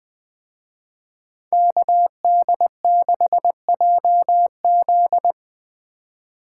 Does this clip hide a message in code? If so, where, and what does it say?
Yes: Morse code from 1.5 to 5.3 s, heard as a keyed tone, about 20 wpm, 700 Hz. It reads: KD6JZ